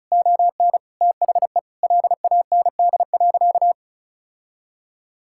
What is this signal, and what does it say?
Morse code, heard as a keyed tone, about 36 wpm, 700 Hz. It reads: ON THE LAND.
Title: Sentences from Top 1000 Words - Review Part 5 - 35wpm